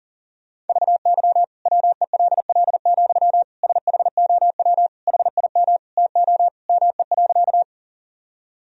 Morse code, heard as a keyed tone, about 40 wpm, 700 Hz. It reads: VY WELL, SHOW HIM TO ME.
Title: Sentences from Top 200 Words and Abbreviations 40wpm